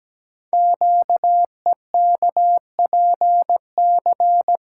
Morse code, heard as a keyed tone, about 17 wpm, 700 Hz. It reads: QEKPC
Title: Code Groups - 5 Letters - 17wpm